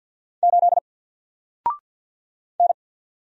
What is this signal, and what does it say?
Morse code, heard as a keyed tone, about 50 wpm, 700 Hz. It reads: 8 N
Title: Single Character (Letter:Number:Punctuation:Prosign) 50wpm